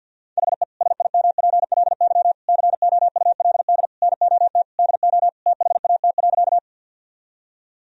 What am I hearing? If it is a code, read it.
Morse code, heard as a keyed tone, about 52 wpm, 700 Hz. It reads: HE SIMPLY COULD NOT DO THAT.
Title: Sentences from Top 1000 Words - Review Part 4 - 50wpm